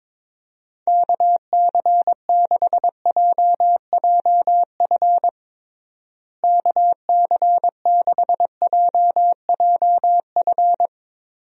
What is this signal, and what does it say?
Morse code, heard as a keyed tone, about 22 wpm, 700 Hz. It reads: KC6JJF KC6JJF
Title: USA Call Signs - 6 Characters - Repeated 2x - 22wpm